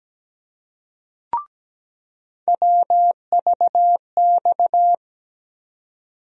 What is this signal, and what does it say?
Morse code, heard as a keyed tone, about 17 wpm, 700 Hz. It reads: WVX
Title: USA Call Sign Postfixes - 3 Letters 17wpm